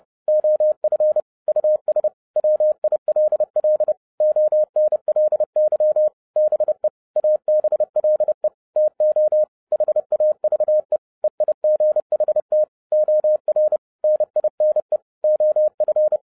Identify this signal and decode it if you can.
Morse code, heard as a keyed tone, about 30 wpm, 610 Hz. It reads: OF US WILL ONLY BE ABLE TO HAVE EIGHT OR NINE OF
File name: 2012-01-mp3 / n11-I DIDNT MISS THE SOLAR CYCLE 24-30wpm